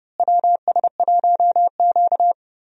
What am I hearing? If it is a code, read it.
Morse code, heard as a keyed tone, about 30 wpm, 700 Hz. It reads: WS1Q
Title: USA Call Signs - 4 Characters - Repeated 3x - 30wpm